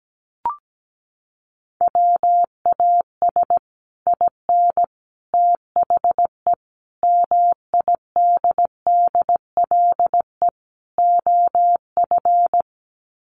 Morse code, heard as a keyed tone, about 17 wpm, 700 Hz. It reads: WAS IN THE MIDDLE OF
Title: Sets of 5 Words Encore 17wpm